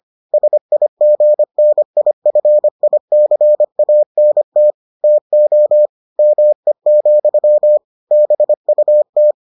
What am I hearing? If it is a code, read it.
Morse code, heard as a keyed tone, about 25 wpm, 600 Hz. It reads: SIGNIFICANT TO ME, BUT